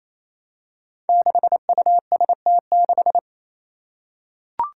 Morse code, heard as a keyed tone, about 28 wpm, 700 Hz. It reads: 6UST6
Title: Five Challenging Letter-Number Combinations 28wpm